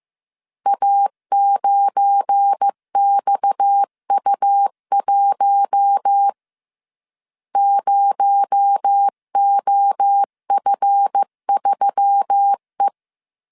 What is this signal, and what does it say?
Morse code, heard as a keyed tone, about 15 wpm, 800 Hz. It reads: A9XU1 0OF3E
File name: CW / DET / 1 9 1500C S A 14wpm 3222